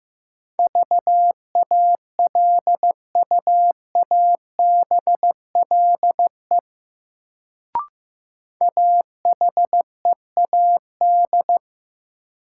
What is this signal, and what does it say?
Morse code, heard as a keyed tone, about 15 wpm, 700 Hz. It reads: VALUABLE AHEAD
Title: Sentences from Top 1100 Words - New Word Drill Alt - 15wpm